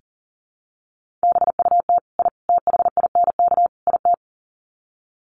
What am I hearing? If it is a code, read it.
Morse code, heard as a keyed tone, about 40 wpm, 700 Hz. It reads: BUT I THINK IT
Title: Sets of 4 Words 40wpm